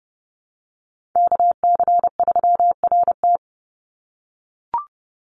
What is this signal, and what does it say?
Morse code, heard as a keyed tone, about 30 wpm, 700 Hz. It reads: KC3RT
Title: NA SOTA Chaser Callsigns 30wpm